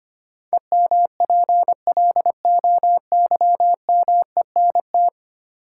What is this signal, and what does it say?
Morse code, heard as a keyed tone, about 25 wpm, 700 Hz. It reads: EMPLOYMENT
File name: 10 Letter Words - First 3 Letters Given - 25wpm